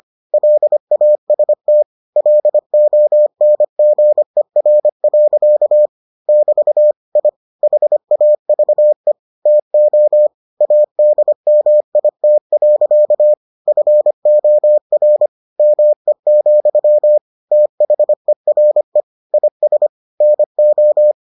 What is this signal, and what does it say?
Morse code, heard as a keyed tone, about 25 wpm, 600 Hz. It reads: LAST LONGER. <BT> I HAVE TO ADMIT. FOR ME, THERE IS NO